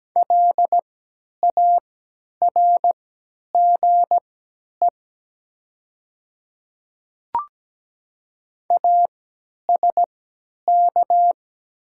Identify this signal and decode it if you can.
Morse code, heard as a keyed tone, about 17 wpm, spaced as for 10 wpm, 700 Hz. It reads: LARGE ASK
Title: 3x Character Spacing - Top 200 Words - 17wpm